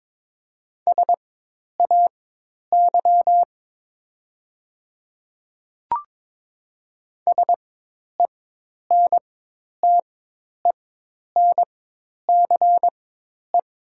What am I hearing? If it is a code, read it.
Morse code, heard as a keyed tone, about 22 wpm, spaced as for 10 wpm, 690 Hz. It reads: SAY SENTENCE
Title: 4x Character Spacing - Top 200 Words - 22wpm